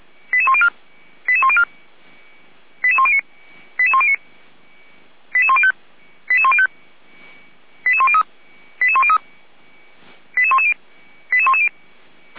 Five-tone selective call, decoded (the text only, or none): 88104 88104 88109 88109 88105 88105 88103 88103 88110 88110